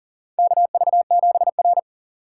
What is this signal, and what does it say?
Morse code, heard as a keyed tone, about 40 wpm, 700 Hz. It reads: KV7R
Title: USA Call Signs - 4 Characters - 6x Character Spacing - 40wpm